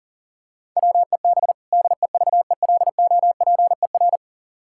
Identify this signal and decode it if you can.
Morse code, heard as a keyed tone, about 40 wpm, 700 Hz. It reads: WEB DEVELOPER
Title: Practice QSO - Exchange 3 - More Info - 40wpm